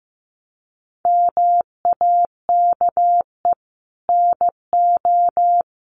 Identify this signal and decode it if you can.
Morse code, heard as a keyed tone, about 14 wpm, 700 Hz. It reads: MAKE NO
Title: Sentences from Top 1500 Words 15wpm